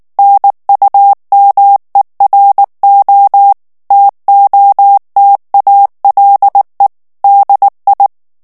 Morse code, heard as a keyed tone, about 19 wpm, 800 Hz. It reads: NUMERO TOTALE DI
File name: morse mp3 / ITALIEN texte clair 2 vitesse 18 mots minute